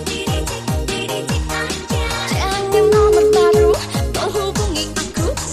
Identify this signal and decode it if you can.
Ringback tone (Europe, 425 Hz), with noise.